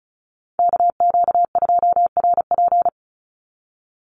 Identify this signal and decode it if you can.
Morse code, heard as a keyed tone, about 34 wpm, 700 Hz. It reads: KQ2RP
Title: NA SOTA Chaser Callsigns 35wpm